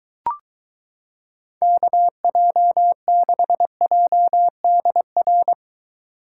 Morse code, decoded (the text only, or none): KJ6JDR